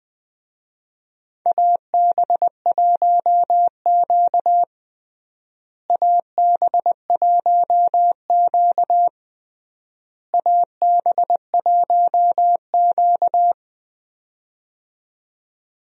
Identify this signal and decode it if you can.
Morse code, heard as a keyed tone, about 20 wpm, 700 Hz. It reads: AB1Q AB1Q AB1Q
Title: USA Call Signs - 4 Characters - Repeated 3x - 20wpm